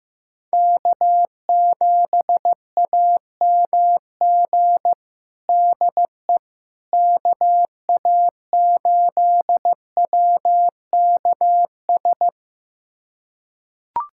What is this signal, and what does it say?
Morse code, heard as a keyed tone, about 16 wpm, 700 Hz. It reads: K7AMG DE KA8WKS
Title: USA Callsign DE Callsign 15wpm